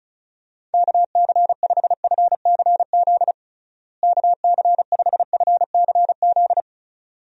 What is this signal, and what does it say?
Morse code, heard as a keyed tone, about 34 wpm, 700 Hz. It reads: KC5FCZ KC5FCZ
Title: USA Call Signs - 6 Characters - Repeated 2x - 35wpm